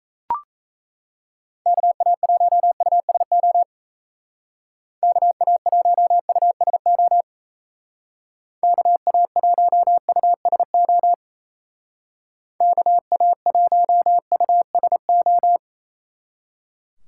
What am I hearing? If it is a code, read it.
Morse code, heard as a keyed tone, about 35 wpm, 700 Hz. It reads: KA1USO KA1USO KA1USO KA1USO
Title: Speed Racing - USA Call Signs 28wpm